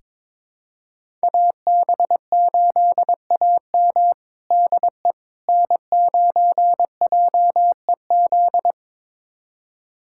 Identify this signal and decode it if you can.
Morse code, heard as a keyed tone, about 22 wpm, 700 Hz. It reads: AB8AM DE N9JEZ
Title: USA Callsign DE Callsign - 5 Characters - 22wpm